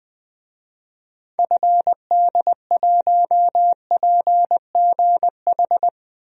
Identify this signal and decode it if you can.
Morse code, heard as a keyed tone, about 21 wpm, 700 Hz. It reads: FD1PGH
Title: DXCC-100 Call Signs - 6 Characters - 20wpm